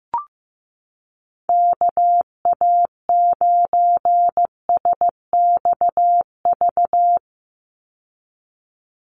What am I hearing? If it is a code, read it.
Morse code, heard as a keyed tone, about 15 wpm, 700 Hz. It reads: KA9SXV
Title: Call Signs Repeated 3x - 15wpm